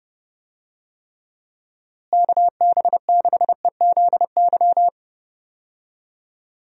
Morse code, heard as a keyed tone, about 32 wpm, 700 Hz. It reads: KB6EZY
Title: USA Call Signs - 6 Characters - Repeated 3x - 30wpm